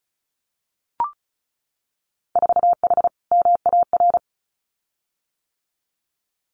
Morse code, heard as a keyed tone, about 36 wpm, 700 Hz. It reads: 4H MAR
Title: Winter Field Day Exchanges Encore 35wpm